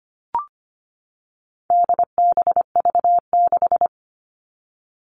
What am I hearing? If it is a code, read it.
Morse code, heard as a keyed tone, about 25 wpm, 700 Hz. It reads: DBV6
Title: All Cut Numbers - 4-Digit 25wpm